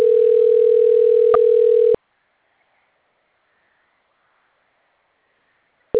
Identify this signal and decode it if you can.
Ringback tone (North America), 440 + 480 Hz.